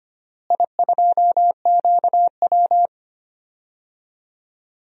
Morse code, heard as a keyed tone, about 25 wpm, 700 Hz. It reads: I2QW
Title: DXCC-100 Call Signs - 4 Characters - Repeated 3x - 25wpm